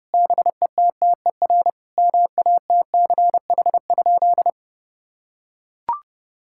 Morse code, heard as a keyed tone, about 30 wpm, 700 Hz. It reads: BETTER MATCH?
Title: Sentences from Top 1600 Words 30wpm